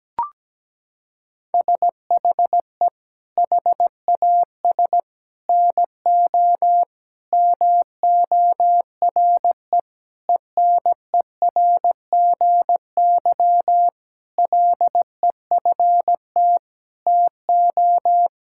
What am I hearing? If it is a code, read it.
Morse code, heard as a keyed tone, about 17 wpm, 700 Hz. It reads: SHE HAS NO MORE ENERGY LEFT TO
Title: Sentences from Top 1000 Words - Review Part 6 - 17wpm